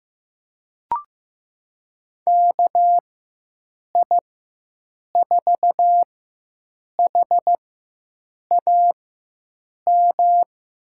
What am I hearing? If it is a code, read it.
Morse code, heard as a keyed tone, about 15 wpm, spaced as for 7 wpm, 700 Hz. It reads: KI4HAM
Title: USA Call Signs - 6 Characters - 4x Character Spacing - 15wpm